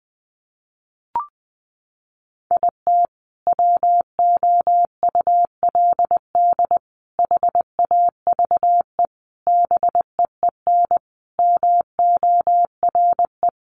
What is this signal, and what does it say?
Morse code, heard as a keyed tone, about 20 wpm, 700 Hz. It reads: IT WOULD HAVE BEEN MORE
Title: Sentences from Top 400 Words 20wpm